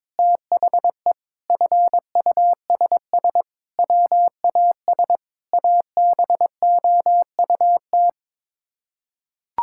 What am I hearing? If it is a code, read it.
Morse code, heard as a keyed tone, about 22 wpm, 700 Hz. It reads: THE FUSS WAS ABOUT
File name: practice / 22wpm / Sets of 8 Words - 22wpm